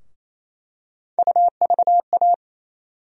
Morse code, heard as a keyed tone, about 28 wpm, 700 Hz. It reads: UVA
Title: USA Call Sign Postfixes - 3 Letters 28wpm